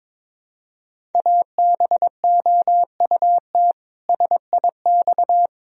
Morse code, heard as a keyed tone, about 21 wpm, 700 Hz. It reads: ABOUT SIX
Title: Sentences from Top 1000 Words - Review Part 2 - 22wpm